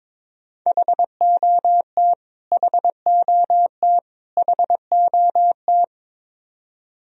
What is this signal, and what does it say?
Morse code, heard as a keyed tone, about 22 wpm, 700 Hz. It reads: HOT HOT HOT
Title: Top 100 Words Repeated 22wpm